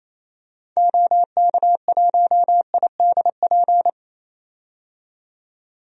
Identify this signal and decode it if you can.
Morse code, heard as a keyed tone, about 28 wpm, 700 Hz. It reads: OK1IDP
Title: DXCC-100 Call Signs - 6 Characters - 28wpm